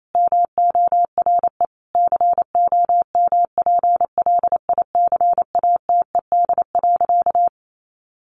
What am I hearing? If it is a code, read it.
Morse code, heard as a keyed tone, about 28 wpm, 700 Hz. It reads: MORE COMPLICATED.
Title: Sentences from Top 1200 Words 28wpm